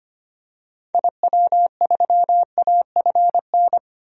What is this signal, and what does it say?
Morse code, heard as a keyed tone, about 25 wpm, 700 Hz. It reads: IW3AFN
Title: DXCC-100 Call Signs - 6 Characters - 25wpm